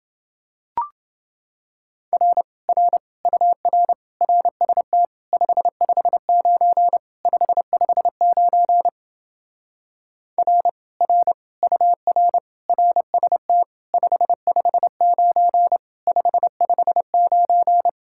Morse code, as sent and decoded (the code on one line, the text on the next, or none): .-. / .-. / ..- .-. / .-. ... - / ..... ..... ----. / ..... ..... ----. / .-. / .-. / ..- .-. / .-. ... - / ..... ..... ----. / ..... ..... ----.
R R UR RST 559 559 R R UR RST 559 559